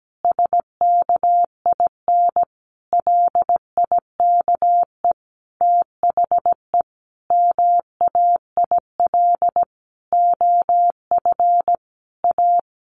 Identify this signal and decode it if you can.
Morse code, heard as a keyed tone, about 17 wpm, 700 Hz. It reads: SKIN LIKE THE MAIL OF A
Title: A Sound Of Thunder 17wpm